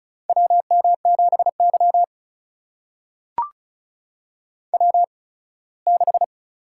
Morse code, heard as a keyed tone, about 36 wpm, spaced as for 21 wpm, 700 Hz. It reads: WM7Y W6
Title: USA Call Signs - 4 Characters - 8x Character Spacing - 35wpm